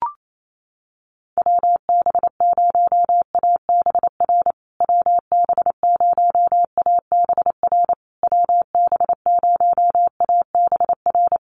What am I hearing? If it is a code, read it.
Morse code, heard as a keyed tone, about 28 wpm, 700 Hz. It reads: WB0ABR WB0ABR WB0ABR
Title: USA Call Signs Repeated 3x 28wpm